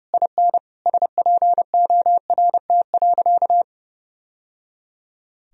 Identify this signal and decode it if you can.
Morse code, heard as a keyed tone, about 30 wpm, 700 Hz. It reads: IN SPORT.
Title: Sentences from Top 1000 Words - Review Part 4 - 30wpm